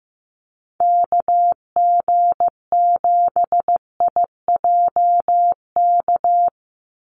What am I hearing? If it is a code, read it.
Morse code, heard as a keyed tone, about 15 wpm, 700 Hz. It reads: KG7IJK
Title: USA Call Signs 15wpm